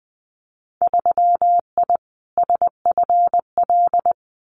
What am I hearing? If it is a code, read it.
Morse code, heard as a keyed tone, about 20 wpm, 700 Hz. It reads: 3I SFL
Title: Winter Field Day Exchanges 20wpm